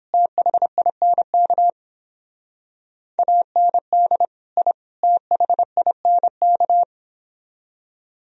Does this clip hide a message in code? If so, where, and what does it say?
Yes: Morse code from 0.1 to 6.8 s, heard as a keyed tone, about 30 wpm, 700 Hz. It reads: THINK AND I THINK